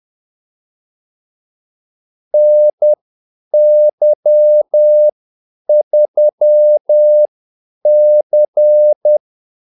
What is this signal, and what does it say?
Morse code, heard as a keyed tone, about 10 wpm, spaced as for 8 wpm, 600 Hz. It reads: NY3C
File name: Callsigns-10-8wpm